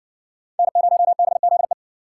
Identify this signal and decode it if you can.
Morse code, heard as a keyed tone, about 60 wpm, 700 Hz. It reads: N0BZE